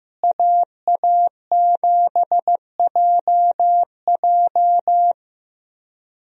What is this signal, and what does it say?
Morse code, heard as a keyed tone, about 15 wpm, 700 Hz. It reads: AA7JJ
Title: USA Call Signs - 5 Characters - Repeated 3x - 15wpm